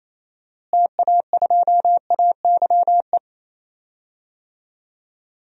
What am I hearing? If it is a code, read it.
Morse code, heard as a keyed tone, about 28 wpm, 700 Hz. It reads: TA2AYE
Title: DXCC-100 Call Signs - 6 Characters - Repeated 2x - 28wpm